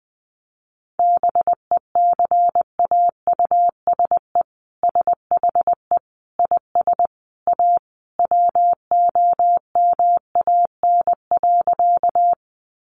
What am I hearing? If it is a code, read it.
Morse code, heard as a keyed tone, about 20 wpm, 700 Hz. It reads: BECAUSE SHE IS A WOMAN.
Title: Sentences from Top 1100 Words 20wpm